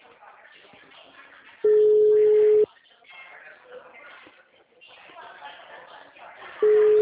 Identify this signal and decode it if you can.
Ringback tone (Europe, 425 Hz), with speech.